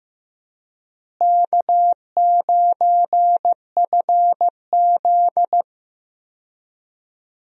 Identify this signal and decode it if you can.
Morse code, heard as a keyed tone, about 15 wpm, 700 Hz. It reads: K9FZ
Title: USA Call Signs - 4 Characters - Repeated 2x - 15wpm